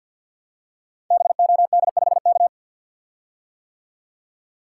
Morse code, heard as a keyed tone, about 50 wpm, 700 Hz. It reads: BOD5K
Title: Code Groups - 5 Letter:Numbers - 50wpm